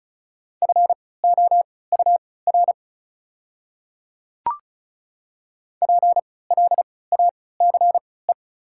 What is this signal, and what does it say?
Morse code, heard as a keyed tone, about 34 wpm, spaced as for 20 wpm, 690 Hz. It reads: FOUR PLACE
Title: 3x Character Spacing - Top 300 Words - 35wpm